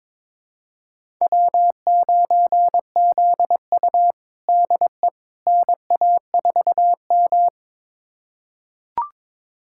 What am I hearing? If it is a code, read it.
Morse code, heard as a keyed tone, about 22 wpm, 700 Hz. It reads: W9ZU DE NA4M
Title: USA Callsign DE Callsign - 4 Characters - 22wpm